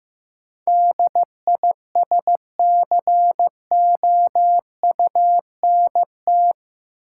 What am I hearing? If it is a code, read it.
Morse code, heard as a keyed tone, about 15 wpm, 700 Hz. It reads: DISCOUNT
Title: Sentences from Top 1200 Words - New Word Drill Alt - 15wpm